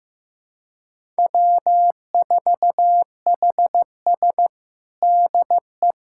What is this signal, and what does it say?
Morse code, heard as a keyed tone, about 16 wpm, 700 Hz. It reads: W4HS DE
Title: USA Callsign DE Callsign - 4 Characters - 15wpm